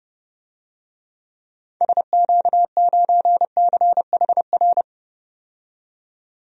Morse code, heard as a keyed tone, about 29 wpm, 700 Hz. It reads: SQ9CHR
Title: DXCC-100 Call Signs - 6 Characters - 30wpm